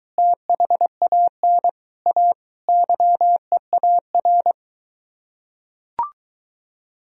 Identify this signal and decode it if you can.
Morse code, heard as a keyed tone, about 22 wpm, 700 Hz. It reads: THAN A YEAR